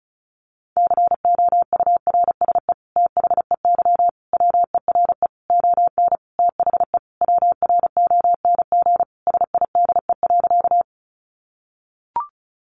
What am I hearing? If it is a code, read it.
Morse code, heard as a keyed tone, about 34 wpm, 700 Hz. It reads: COURSE THEY WERE ON THE WRONG SIDE.
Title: Sentences from Top 900 Words 35wpm